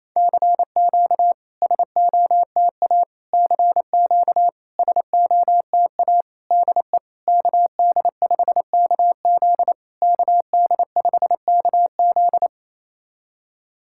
Morse code, heard as a keyed tone, about 28 wpm, 700 Hz. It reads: CQ SOTA CQ SOTA DE KD5KZ KD5KZ
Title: SOTA QSO Practice 28wpm